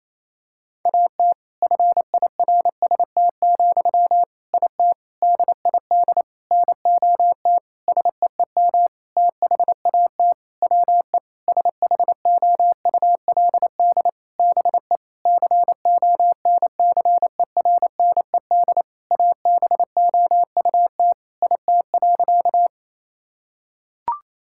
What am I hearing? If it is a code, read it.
Morse code, heard as a keyed tone, about 28 wpm, 700 Hz. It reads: AT FIRST, IT DID NOT SEEM THAT WE SHOULD BE CONCERNED ABOUT IT.
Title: Sentences from Top 1000 Words - Review Part 6 - 28wpm